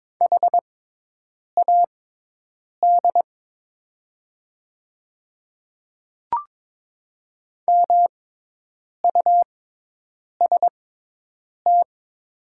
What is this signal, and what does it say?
Morse code, heard as a keyed tone, about 22 wpm, spaced as for 8 wpm, 700 Hz. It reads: HAD MUST
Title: 6x Character Spacing - Top 300 Words - 22wpm